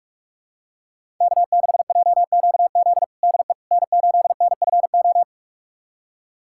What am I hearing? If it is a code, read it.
Morse code, heard as a keyed tone, about 45 wpm, 700 Hz. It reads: K6JQZ DE N8NFO
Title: USA Callsign DE Callsign - 5 Characters - 45wpm